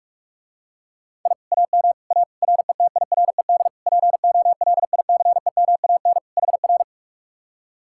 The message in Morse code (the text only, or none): I AM A RETIRED POLICEMAN HR